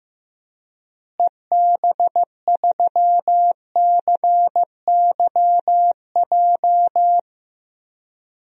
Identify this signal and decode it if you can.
Morse code, heard as a keyed tone, about 15 wpm, 700 Hz. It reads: EB3CYJ
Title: DXCC-100 Call Signs - 6 Characters - Repeated 2x - 15wpm